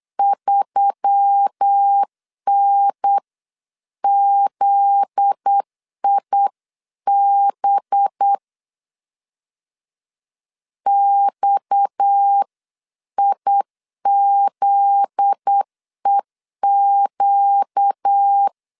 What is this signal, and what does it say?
Morse code, heard as a keyed tone, about 8 wpm, 800 Hz. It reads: 3NZIB XIZEQ
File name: LEZIONI / 7 GF9Z / 7 GF9Z 8wpm 3213